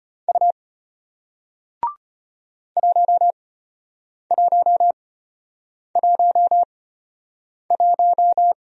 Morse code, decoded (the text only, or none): U 1111